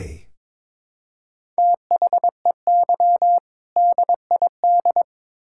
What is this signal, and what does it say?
Morse code, heard as a keyed tone, about 21 wpm, 700 Hz. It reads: THEY DID
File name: practice / 22wpm / Build A Phrase - 5-Word Phrases - 22wpm